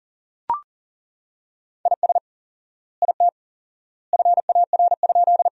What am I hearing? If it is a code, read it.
Morse code, heard as a keyed tone, about 37 wpm, 700 Hz. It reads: IS IT FAR?
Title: 4x Word-Spacing - Sentences from Top 300 Words - 40wpm